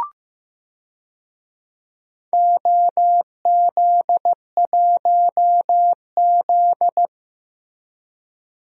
Morse code, heard as a keyed tone, about 15 wpm, 700 Hz. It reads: OZ1Z
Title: DXCC-100 Call Signs - 4 Characters - Repeated 2x - 15wpm